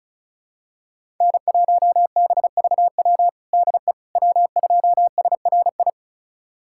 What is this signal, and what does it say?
Morse code, heard as a keyed tone, about 36 wpm, 690 Hz. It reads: N1BVW DE W2SRI
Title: USA Callsign DE Callsign - 5 Characters - 35wpm